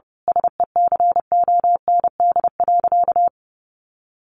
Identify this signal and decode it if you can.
Morse code, heard as a keyed tone, about 30 wpm, 700 Hz. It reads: SECOND.
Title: A Sound Of Thunder 30wpm